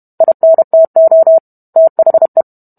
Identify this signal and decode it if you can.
Morse code, heard as a keyed tone, about 32 wpm, 650 Hz. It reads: INTO THE